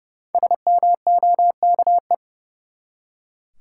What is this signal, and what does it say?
Morse code, heard as a keyed tone, about 30 wpm, 700 Hz. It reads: SMOKE
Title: Five Letter Words - 30wpm